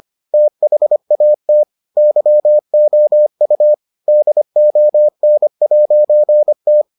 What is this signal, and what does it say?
Morse code, heard as a keyed tone, about 25 wpm, 600 Hz. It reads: THAT YOU DON'T